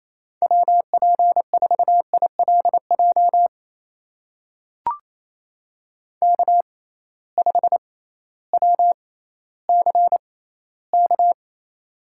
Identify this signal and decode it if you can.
Morse code, heard as a keyed tone, about 28 wpm, spaced as for 15 wpm, 700 Hz. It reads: WP4ILJ K5WCK
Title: USA Call Signs - 6x Character Spacing - 28wpm